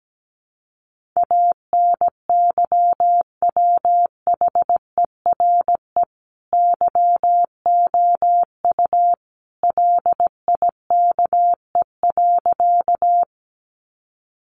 Morse code, heard as a keyed tone, about 17 wpm, 700 Hz. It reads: ANYWHERE YOU LIKE.
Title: Sentences from Top 1400 Words 17wpm